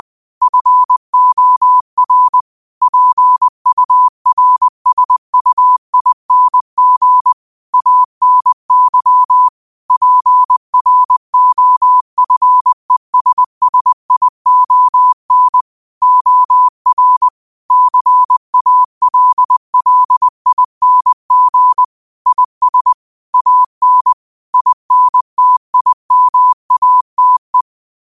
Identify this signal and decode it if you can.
Morse code, heard as a keyed tone, about 20 wpm, 1000 Hz. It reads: FOR PURSUING ANY PROFESSION OR CALLING IS AN INTIMATE
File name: qotdfiles / 20wpm